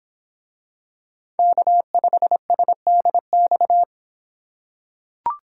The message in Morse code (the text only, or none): K5SDX